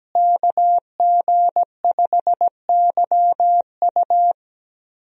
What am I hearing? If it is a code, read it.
Morse code, heard as a keyed tone, about 17 wpm, 700 Hz. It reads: KG5YU
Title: USA Call Signs - 5 Characters - Repeated 2x - 17wpm